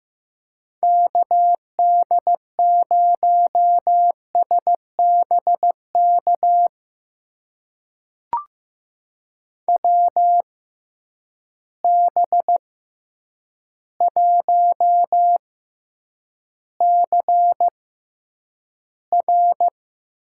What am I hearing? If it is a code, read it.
Morse code, heard as a keyed tone, about 15 wpm, spaced as for 8 wpm, 700 Hz. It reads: KD0SBK WB1CR
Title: USA Call Signs - 6 Characters - 6x Character Spacing - 15wpm